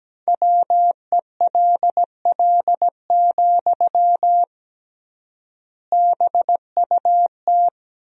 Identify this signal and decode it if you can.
Morse code, heard as a keyed tone, about 17 wpm, 700 Hz. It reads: WELL, BUT
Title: 3x Word-Spacing - Sentences from Top 500 Words - 17wpm